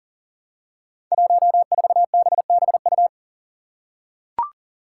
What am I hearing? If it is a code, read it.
Morse code, heard as a keyed tone, about 40 wpm, 700 Hz. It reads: 14BBU
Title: Five Challenging Letter-Number Combinations 40wpm